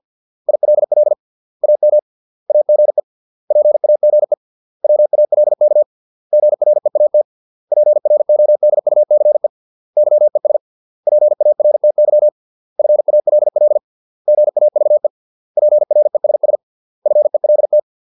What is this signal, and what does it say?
Morse code, heard as a keyed tone, about 50 wpm, 600 Hz. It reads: ILL AM AGE PAGE WALK GREAT PRODUCE YES PARTY FALL GAVE PRESS FELT